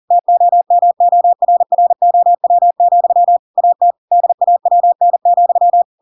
Morse code, heard as a keyed tone, about 40 wpm, 700 Hz. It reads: TOMORROW, AT DAWN,
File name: Pinocchio / Pinocchio ch4 40 40 ik1hge